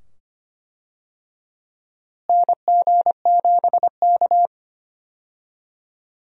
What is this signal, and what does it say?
Morse code, heard as a keyed tone, about 25 wpm, 700 Hz. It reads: NG7K
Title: USA Call Signs - 4 Characters - Repeated 2x - 25wpm